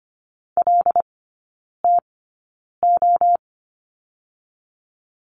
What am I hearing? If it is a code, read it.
Morse code, heard as a keyed tone, about 25 wpm, spaced as for 9 wpm, 700 Hz. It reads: LTO
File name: Lesson 17e - Letter u - Three Characters 25wpm